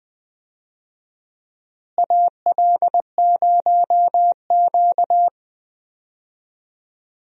Morse code, heard as a keyed tone, about 20 wpm, 700 Hz. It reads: AL0Q
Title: USA Call Signs - 4 Characters - Repeated 3x - 20wpm